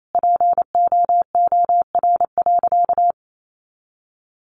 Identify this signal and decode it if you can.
Morse code, heard as a keyed tone, about 28 wpm, 700 Hz. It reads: POOR.
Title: Sentences from Top 1100 Words 28wpm